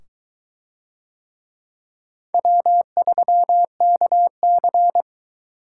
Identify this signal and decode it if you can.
Morse code, heard as a keyed tone, about 21 wpm, 700 Hz. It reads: W3KC